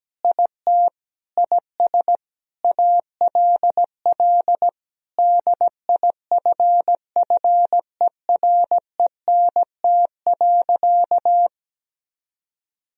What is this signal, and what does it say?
Morse code, heard as a keyed tone, about 17 wpm, 700 Hz. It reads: IT IS ALL DIFFERENT.
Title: Sentences from Top 1000 Words - Review Part 3 - 17wpm